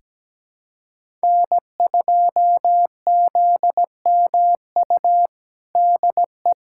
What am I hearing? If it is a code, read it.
Morse code, heard as a keyed tone, about 17 wpm, 700 Hz. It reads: N2ZMU DE
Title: USA Callsign DE Callsign 17wpm